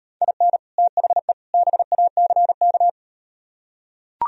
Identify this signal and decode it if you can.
Morse code, heard as a keyed tone, about 38 wpm, 700 Hz. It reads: IN THE BACK